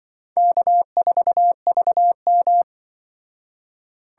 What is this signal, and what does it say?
Morse code, heard as a keyed tone, about 24 wpm, 700 Hz. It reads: K4VM
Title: Speed Racing - USA 1x2 Call Signs 20wpm